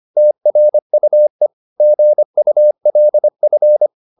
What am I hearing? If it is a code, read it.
Morse code, heard as a keyed tone, about 25 wpm, 600 Hz. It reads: TRUE GULF